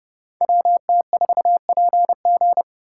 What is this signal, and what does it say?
Morse code, heard as a keyed tone, about 29 wpm, 700 Hz. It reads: WT4PG